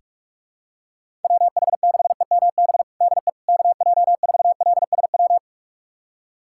Morse code, heard as a keyed tone, about 47 wpm, 700 Hz. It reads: WH6EMB DE KJ4LSW
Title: USA Callsign DE Callsign 45wpm